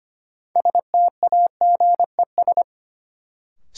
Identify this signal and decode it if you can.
Morse code, heard as a keyed tone, about 25 wpm, 700 Hz. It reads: STAGES
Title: Sets of 5 Words Encore - New Word Drill - 25wpm